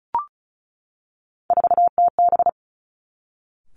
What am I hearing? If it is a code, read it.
Morse code, heard as a keyed tone, about 36 wpm, 700 Hz. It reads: 4TB
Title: Three Challenging Letter-Number Combinations 35wpm